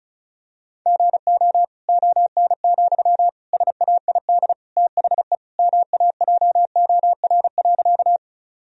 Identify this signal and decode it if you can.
Morse code, heard as a keyed tone, about 36 wpm, 700 Hz. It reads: GO ON, SAID THE MAJOR.
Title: Sentences from Top 1000 Words - Review Part 4 - 35wpm